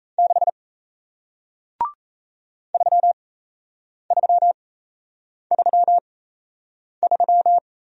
Morse code, heard as a keyed tone, about 38 wpm, spaced as for 8 wpm, 700 Hz. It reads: 6 3333